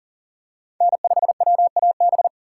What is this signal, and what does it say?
Morse code, heard as a keyed tone, about 40 wpm, 700 Hz. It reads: N5WAB
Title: USA Call Signs - 3x Character Spacing - 40wpm